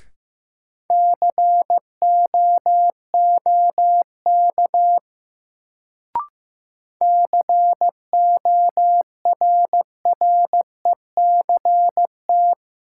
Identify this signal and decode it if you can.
Morse code, heard as a keyed tone, about 15 wpm, 700 Hz. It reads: COOK CORRECT
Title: Sentences from Top 700 Words - New Word Drill - 15wpm